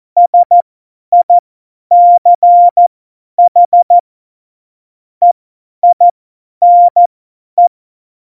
Morse code, heard as a keyed tone, about 14 wpm, spaced as for 10 wpm, 700 Hz. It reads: SICH EINE
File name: Morsekurs / Witze / Witz 04 Juli 17, Tempo50 F70